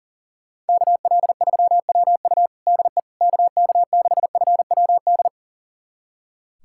Morse code, heard as a keyed tone, about 40 wpm, 700 Hz. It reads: KL3WU DE KK6FWD
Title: USA Callsign DE Callsign 40wpm